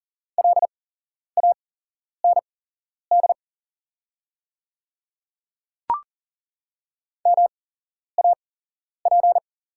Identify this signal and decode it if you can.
Morse code, heard as a keyed tone, about 40 wpm, spaced as for 11 wpm, 700 Hz. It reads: LAND MAP
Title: 8x Character Spacing - Top 500 Words - 40wpm